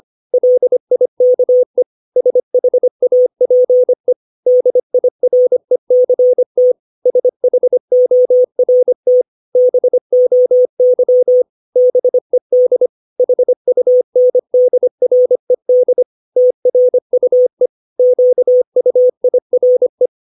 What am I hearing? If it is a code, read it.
Morse code, heard as a keyed tone, about 25 wpm, 500 Hz. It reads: LIKE SHAPE DIRECT SHORT BOY BED HUNDRED TRUE QUIRE
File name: 632 words / 632 Words 25WPM